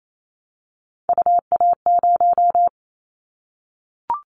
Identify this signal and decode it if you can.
Morse code, heard as a keyed tone, about 28 wpm, 700 Hz. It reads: UA0
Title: Callsign Prefixes for Top 25 DXCC - Repeated 3x - 28wpm